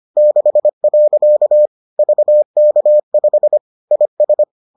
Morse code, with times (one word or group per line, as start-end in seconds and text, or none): 0.2-1.7 6.
2.0-3.6 VK5
3.9-4.4 IS